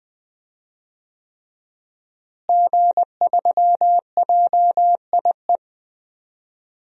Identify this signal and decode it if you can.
Morse code, heard as a keyed tone, about 20 wpm, 700 Hz. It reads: G3JIE
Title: DXCC-100 Call Signs - 5 Characters - Repeated 3x - 20wpm